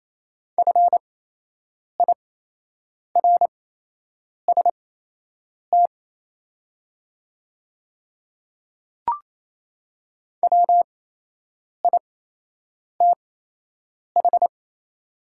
Morse code, heard as a keyed tone, about 28 wpm, spaced as for 8 wpm, 700 Hz. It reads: FIRST WITH
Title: 8x Character Spacing - Top 100 Words - 28wpm